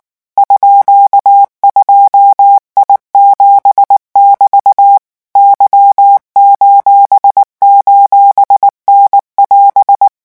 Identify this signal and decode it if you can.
Morse code, heard as a keyed tone, about 19 wpm, 800 Hz. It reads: _2I7<BT> Y::N<AS>
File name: morse mp3 / revision19-vitesse 18 mots